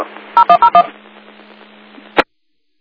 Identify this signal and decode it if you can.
DTMF, heard as telephone touch tones, tones of about 80 ms. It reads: 0101